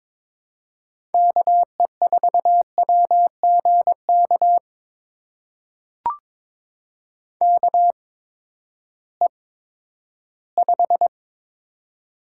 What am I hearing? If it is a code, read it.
Morse code, heard as a keyed tone, about 22 wpm, spaced as for 12 wpm, 700 Hz. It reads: KE4WGK KE5
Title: USA Call Signs - 6 Characters - 8x Character Spacing - 22wpm